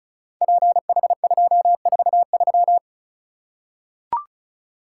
Morse code, heard as a keyed tone, about 36 wpm, 700 Hz. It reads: PH243
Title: Code Groups - 5 Letter:Numbers - 35wpm